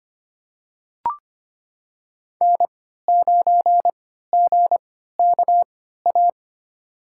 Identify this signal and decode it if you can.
Morse code, heard as a keyed tone, about 25 wpm, spaced as for 14 wpm, 700 Hz. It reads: N9GKA
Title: USA Call Signs - 5 Characters - 3x Character Spacing - 25wpm